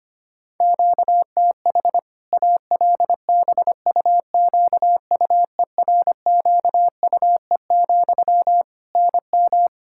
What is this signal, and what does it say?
Morse code, heard as a keyed tone, about 25 wpm, 700 Hz. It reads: QTH ALBUQUERQUE, NM
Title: Practice QSO - Exchange 1 - Beginning - 25wpm